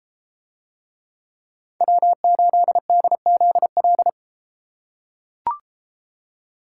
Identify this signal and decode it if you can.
Morse code, heard as a keyed tone, about 32 wpm, 700 Hz. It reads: W8DZL